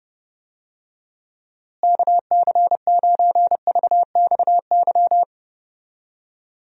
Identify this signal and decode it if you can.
Morse code, heard as a keyed tone, about 32 wpm, 700 Hz. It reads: KC9VXY
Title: USA Call Signs - 6 Characters - Repeated 2x - 30wpm